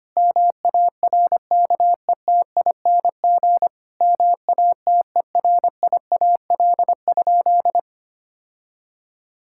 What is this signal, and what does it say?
Morse code, heard as a keyed tone, about 25 wpm, 700 Hz. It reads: MARKETING MATERIAL?
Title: Sentences from Top 1000 Words - Review Part 5 - 25wpm